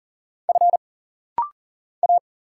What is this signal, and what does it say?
Morse code, heard as a keyed tone, about 40 wpm, 700 Hz. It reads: F A
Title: Single Letters - Rapid-Fire 40wpm